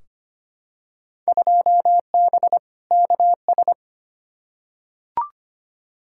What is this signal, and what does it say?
Morse code, heard as a keyed tone, about 25 wpm, 700 Hz. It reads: 2B KS
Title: Field Day Exchanges Encore - 25wpm